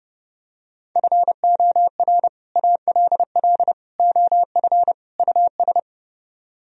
Morse code, heard as a keyed tone, about 31 wpm, 700 Hz. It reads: FOR ALL OF US